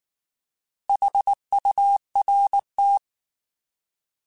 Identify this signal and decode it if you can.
Morse code, heard as a keyed tone, about 19 wpm, 800 Hz. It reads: HURT
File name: Alphabetical / Alphabetical mp3s / S vs H @18